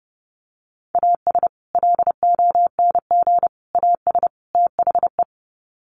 Morse code, heard as a keyed tone, about 30 wpm, 700 Hz. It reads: AS LONG AS THE
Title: Sets of 4 Words 30wpm